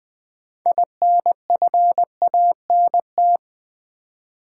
Morse code, heard as a keyed tone, about 20 wpm, 700 Hz. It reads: INFANT
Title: 6 Letter Words - First 3 Letters Given - 20wpm